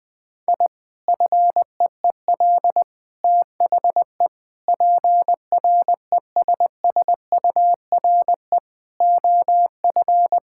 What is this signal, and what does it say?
Morse code, heard as a keyed tone, about 20 wpm, 700 Hz. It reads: I FEEL THE PRESSURE OF
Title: Sentences from Top 1000 Words - Review Part 6 - 20wpm